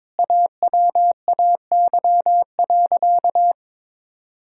Morse code, heard as a keyed tone, about 22 wpm, 700 Hz. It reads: AWAY.
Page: morse-code-ninja-practice-22wpm